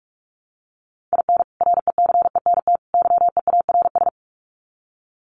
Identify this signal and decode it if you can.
Morse code, heard as a keyed tone, about 44 wpm, 700 Hz. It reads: IN RECENT YEARS
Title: Sets of 3 Words 45wpm